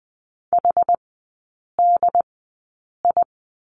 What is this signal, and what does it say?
Morse code, heard as a keyed tone, about 20 wpm, spaced as for 8 wpm, 690 Hz. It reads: HDI
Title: 093 Lesson 16e - Letter c - Three Characters 20wpm